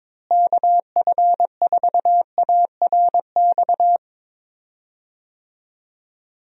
Morse code, heard as a keyed tone, about 22 wpm, 700 Hz. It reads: KF4ARX
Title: USA Call Signs - 6 Characters - Repeated 2x - 22wpm